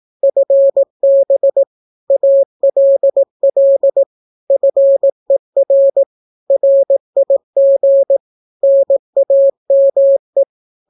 Morse code, heard as a keyed tone, about 18 wpm, 550 Hz. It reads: FB ALL FER RIG NAME
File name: naukapodstaw / skroty / skroty